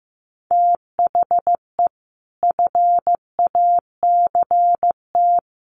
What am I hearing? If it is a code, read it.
Morse code, heard as a keyed tone, about 15 wpm, 700 Hz. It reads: THE FACT
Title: Sets of 5 Words Encore2 15wpm